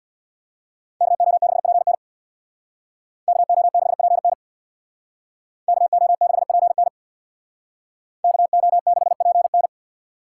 Morse code, heard as a keyed tone, about 65 wpm, 700 Hz. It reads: KO6PN KO6PN KO6PN KO6PN